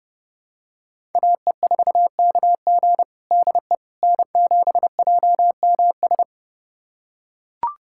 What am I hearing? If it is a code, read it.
Morse code, heard as a keyed tone, about 30 wpm, 700 Hz. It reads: AE4KG DE N7JMS
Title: USA Callsign DE Callsign - 5 Characters - 30wpm